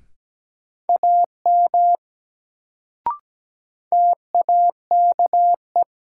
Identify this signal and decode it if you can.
Morse code, heard as a keyed tone, about 17 wpm, 700 Hz. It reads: AM TAKE
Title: Sets of 2 Words - New Word Drill - 17wpm